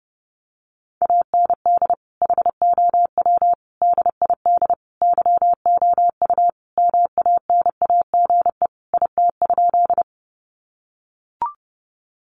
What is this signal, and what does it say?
Morse code, heard as a keyed tone, about 30 wpm, 700 Hz. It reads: AND HOW DID YOU MANAGE IT?
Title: Sentences from Top 1400 Words 30wpm